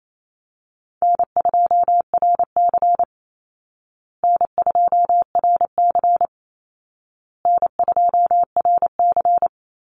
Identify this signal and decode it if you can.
Morse code, heard as a keyed tone, about 28 wpm, 700 Hz. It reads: N2RC N2RC N2RC
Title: Easy Call Signs Repeated 3x - 28wpm